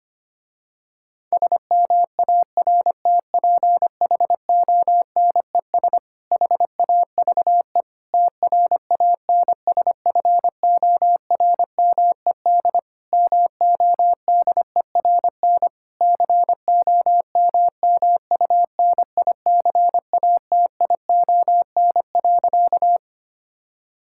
Morse code, heard as a keyed tone, about 25 wpm, 700 Hz. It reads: SMARTPHONES HAVE TRANSFORMED MODERN COMMUNICATION.
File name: practice / 25wpm / Difficult Sentences with Hints - 25wpm